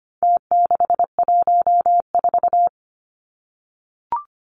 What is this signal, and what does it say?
Morse code, heard as a keyed tone, about 25 wpm, 700 Hz. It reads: T614
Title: Numbers - Contest Style - 4-Digit 25wpm